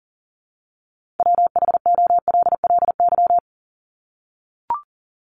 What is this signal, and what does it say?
Morse code, heard as a keyed tone, about 40 wpm, 700 Hz. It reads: WHOLLY
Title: LY Suffix - Top 100 Words with Base Words 40wpm